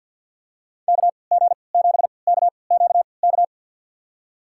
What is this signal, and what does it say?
Morse code, heard as a keyed tone, about 53 wpm, spaced as for 29 wpm, 700 Hz. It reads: KG7KQK
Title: USA Call Signs - 3x Character Spacing - 50wpm